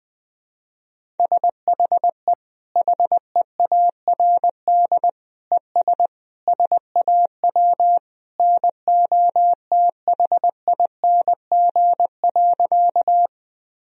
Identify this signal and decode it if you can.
Morse code, heard as a keyed tone, about 19 wpm, 700 Hz. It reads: SHE HEARD ES SAW NOTHING.
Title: Sentences from Top 500 Words and Abbreviations 20wpm